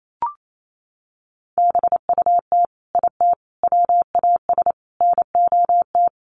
Morse code, heard as a keyed tone, about 28 wpm, 700 Hz. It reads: BUT IT WAS NOT
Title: Sets of 4 Words 28wpm